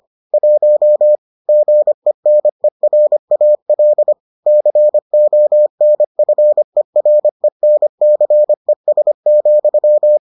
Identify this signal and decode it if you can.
Morse code, heard as a keyed tone, about 25 wpm, 600 Hz. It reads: 1 GENERAL CONFERENCES,